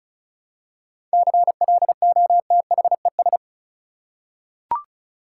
Morse code, heard as a keyed tone, about 36 wpm, 700 Hz. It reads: CLOTHES
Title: Seven Letter Words - 35wpm